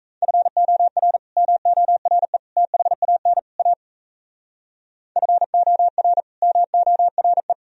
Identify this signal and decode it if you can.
Morse code, heard as a keyed tone, about 34 wpm, 700 Hz. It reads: FOR MORE THAN A FOR MORE
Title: Speed-Racing - Sets of 4 Words 28wpm